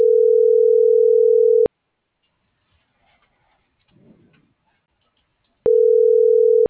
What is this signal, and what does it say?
Ringback tone (North America), 440 + 480 Hz.